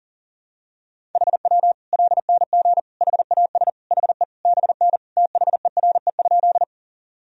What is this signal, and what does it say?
Morse code, heard as a keyed tone, about 37 wpm, 700 Hz. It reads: HW LNG HAS HE BN THERE?
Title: Sentences from Top 100 Words and Abbreviations 40wpm